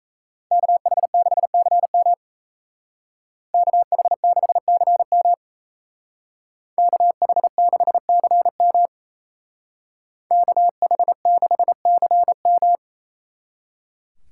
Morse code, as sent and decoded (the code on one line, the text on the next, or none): -.- .... -.... -.-. -- / -.- .... -.... -.-. -- / -.- .... -.... -.-. -- / -.- .... -.... -.-. --
KH6CM KH6CM KH6CM KH6CM